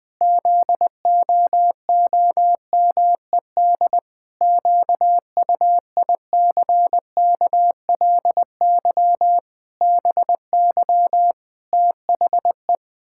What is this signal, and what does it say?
Morse code, heard as a keyed tone, about 20 wpm, 700 Hz. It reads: ZOOMED QUICKLY BY THE
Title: Pangram Challenge v2 - 20wpm